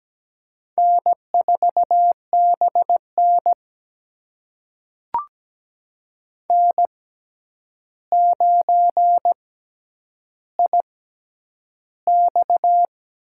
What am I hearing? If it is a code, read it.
Morse code, heard as a keyed tone, about 17 wpm, spaced as for 9 wpm, 700 Hz. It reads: N4BN N9IX
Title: USA Call Signs - 4 Characters - 6x Character Spacing - 17wpm